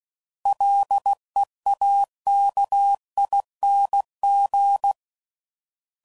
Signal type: Morse code